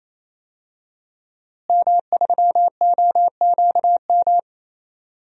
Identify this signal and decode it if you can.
Morse code, heard as a keyed tone, about 28 wpm, 700 Hz. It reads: M3OQM
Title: DXCC-100 Call Signs - 5 Characters - Repeated 3x - 28wpm